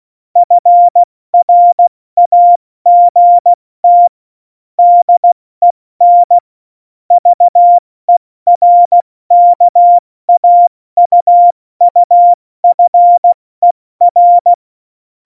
Morse code, heard as a keyed tone, about 16 wpm, 700 Hz. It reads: FRAGT DEN VERKAUUFER
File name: Morsekurs / Witze / Witz 20. Oktober 2020, 70 F80